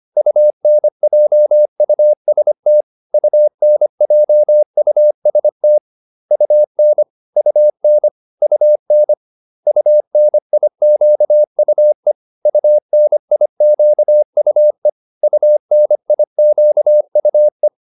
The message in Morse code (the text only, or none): UNJUST UNJUST UN UN UN UNIQUE UNIQUE UNIQUE